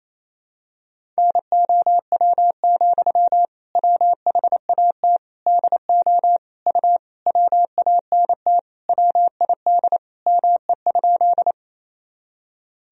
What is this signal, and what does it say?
Morse code, heard as a keyed tone, about 28 wpm, 700 Hz. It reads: NOW, WHAT DO U WANT WID ME?
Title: Sentences from Top 200 Words and Abbreviations 28wpm